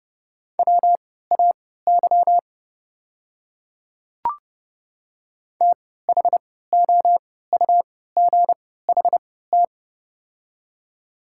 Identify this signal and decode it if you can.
Morse code, heard as a keyed tone, about 30 wpm, spaced as for 17 wpm, 690 Hz. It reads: WAY THOUGHT
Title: 3x Character Spacing - Top 300 Words - 30wpm